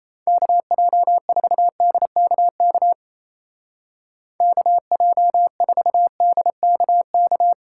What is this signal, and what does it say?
Morse code, heard as a keyed tone, about 34 wpm, 700 Hz. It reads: KJ4DKK KJ4DKK
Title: Speed Racing - USA 2x3 Call Signs 28wpm